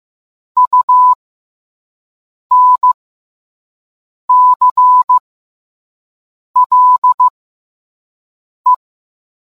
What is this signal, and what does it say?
Morse code, heard as a keyed tone, about 15 wpm, spaced as for 5 wpm, 1000 Hz. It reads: UNCLE